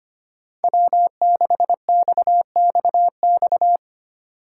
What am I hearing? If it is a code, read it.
Morse code, heard as a keyed tone, about 25 wpm, 700 Hz. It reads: W6XXX
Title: USA Call Signs - 5 Characters - 4x Character Spacing - 25wpm